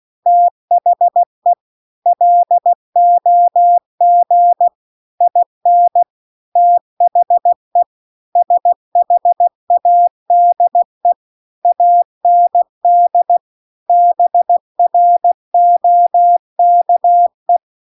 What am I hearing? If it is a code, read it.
Morse code, heard as a keyed tone, about 16 wpm, 700 Hz. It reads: THE LOG IN THE SHADE AND BROKE